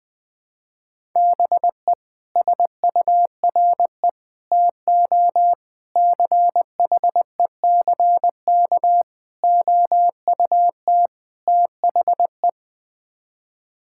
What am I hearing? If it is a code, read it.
Morse code, heard as a keyed tone, about 20 wpm, 700 Hz. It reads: BE SURE TO CHECK OUT THE
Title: Sets of 6 Words - 20wpm